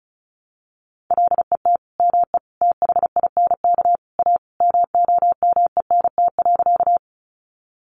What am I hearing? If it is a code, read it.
Morse code, heard as a keyed tone, about 34 wpm, 700 Hz. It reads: LET ME THINK A MOMENT.
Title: Sentences from Top 1000 Words 35wpm